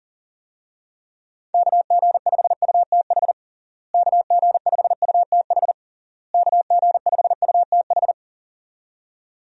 Morse code, heard as a keyed tone, about 40 wpm, 700 Hz. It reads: KG5UTH KG5UTH KG5UTH
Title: USA Call Signs - 6 Characters - Repeated 3x - 40wpm